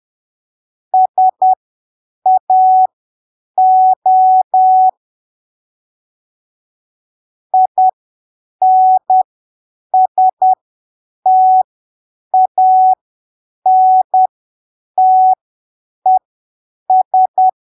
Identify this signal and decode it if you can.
Morse code, heard as a keyed tone, about 10 wpm, spaced as for 7 wpm, 750 Hz. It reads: SAO INSTANTES
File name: arquivos / texto / texto 27 10ppm